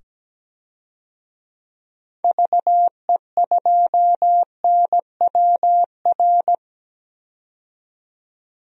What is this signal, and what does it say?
Morse code, heard as a keyed tone, about 17 wpm, 700 Hz. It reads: VE2NWR
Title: DXCC-100 Call Signs - 6 Characters - 17wpm